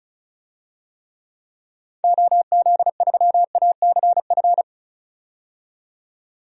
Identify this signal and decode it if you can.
Morse code, heard as a keyed tone, about 36 wpm, 700 Hz. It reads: OZ3ACF
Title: DXCC-100 Call Signs - 6 Characters - 35wpm